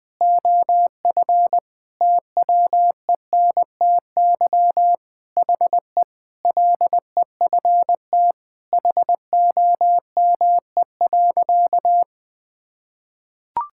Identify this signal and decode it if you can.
Morse code, heard as a keyed tone, about 20 wpm, 700 Hz. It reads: OF TWENTY HE LEFT HOME.
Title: Sentences with 4-Word Phrases - 20wpm